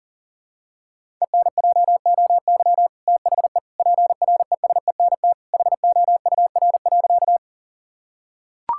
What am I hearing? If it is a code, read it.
Morse code, heard as a keyed tone, about 40 wpm, 700 Hz. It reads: ENJOY THE PRESENT HOUR.
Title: Sentences from Top 1000 Words - Review Part 5 - 40wpm